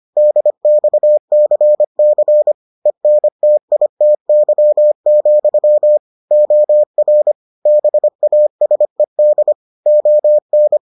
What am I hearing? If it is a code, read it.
Morse code, heard as a keyed tone, about 25 wpm, 600 Hz. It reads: DXCC ENTITY, OR BASED ON